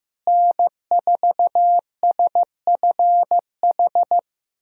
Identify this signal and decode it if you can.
Morse code, heard as a keyed tone, about 16 wpm, 700 Hz. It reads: N4SFH
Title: USA Call Signs - 5 Characters - Repeated 2x - 15wpm